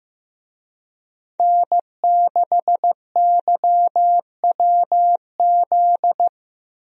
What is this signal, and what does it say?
Morse code, heard as a keyed tone, about 16 wpm, 700 Hz. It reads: N6YWZ
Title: USA Call Signs - 5 Characters - Repeated 2x - 15wpm